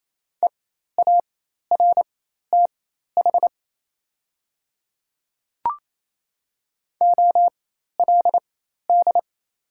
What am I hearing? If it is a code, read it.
Morse code, heard as a keyed tone, about 28 wpm, spaced as for 13 wpm, 700 Hz. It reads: EARTH OLD
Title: 4x Character Spacing - Top 200 Words - 28wpm